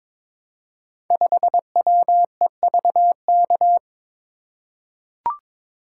Morse code, heard as a keyed tone, about 22 wpm, 700 Hz. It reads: 5WEVK
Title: Code Groups - 5 Letter:Numbers - 22wpm